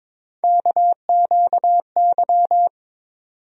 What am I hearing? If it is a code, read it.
Morse code, heard as a keyed tone, about 22 wpm, 700 Hz. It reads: KQY
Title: USA Call Sign Postfixes - 3 Letters 22wpm